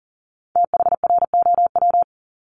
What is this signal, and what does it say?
Morse code, heard as a keyed tone, about 38 wpm, 700 Hz. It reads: THROW
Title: Multi-letter Phonemes Repeated 3x 40wpm